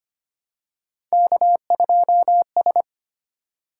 Morse code, heard as a keyed tone, about 25 wpm, 700 Hz. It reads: K2S
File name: Three Letter-Number Combinations - 25wpm